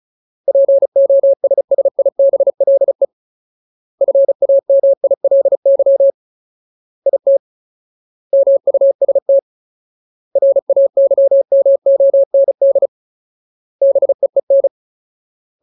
Morse code, heard as a keyed tone, about 34 wpm, 550 Hz. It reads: POSSIBLE FAMILY IT MUST RAYMOND BEEN
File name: Ultimate Word List / 35WPM Ultimate